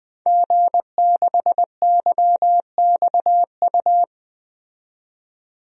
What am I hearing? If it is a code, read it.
Morse code, heard as a keyed tone, about 20 wpm, 700 Hz. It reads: G6YXU
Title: DXCC-100 Call Signs - 5 Characters - 20wpm